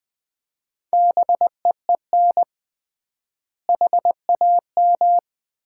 Morse code, heard as a keyed tone, about 20 wpm, 700 Hz. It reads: BEEN HAM
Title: Common QSO Phrases - 3x Word-Spacing - 20wpm